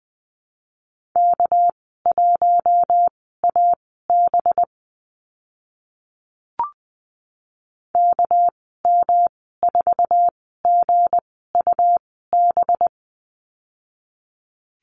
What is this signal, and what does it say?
Morse code, heard as a keyed tone, about 20 wpm, spaced as for 14 wpm, 700 Hz. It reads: K1AB KM4GUB
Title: 207 Lesson 33g - Letter j - Call Signs 20wpm